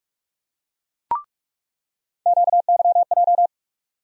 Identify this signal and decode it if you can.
Morse code, heard as a keyed tone, about 47 wpm, 700 Hz. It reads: QYJ
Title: USA Call Sign Postfixes - 3 Letters 45wpm